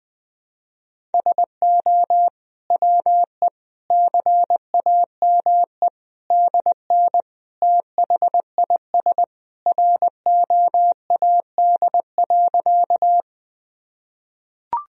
Morse code, heard as a keyed tone, about 20 wpm, 700 Hz. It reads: SO WE CAME DN THIS ROAD.
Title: Sentences from Top 400 Words and Abbreviations 20wpm